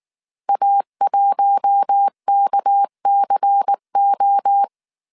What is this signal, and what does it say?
Morse code, heard as a keyed tone, about 19 wpm, 800 Hz. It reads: A1KCO
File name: CW / DET / 1 6 1500C S A 18wpm 2255